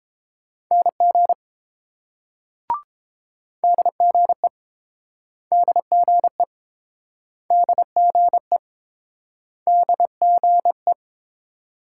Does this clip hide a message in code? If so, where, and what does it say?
Yes: Morse code from 0.7 to 10.9 s, heard as a keyed tone, about 29 wpm, 700 Hz. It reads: NG DGE DGE DGE DGE